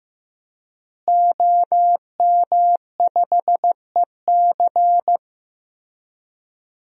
Morse code, heard as a keyed tone, about 16 wpm, 700 Hz. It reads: OM5EC
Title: DXCC-100 Call Signs - 5 Characters - Repeated 3x - 15wpm